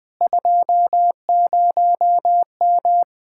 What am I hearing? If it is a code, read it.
Morse code, heard as a keyed tone, about 20 wpm, 700 Hz. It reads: 20M
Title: Speed-Racing Common QSO Phrases - 20wpm